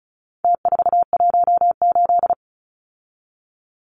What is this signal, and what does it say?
Morse code, heard as a keyed tone, about 34 wpm, 700 Hz. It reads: T418
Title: Numbers - Contest Style - 4-Digit 35wpm